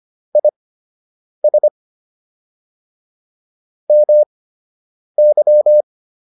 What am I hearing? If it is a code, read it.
Morse code, heard as a keyed tone, about 25 wpm, spaced as for 8 wpm, 600 Hz. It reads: IS MY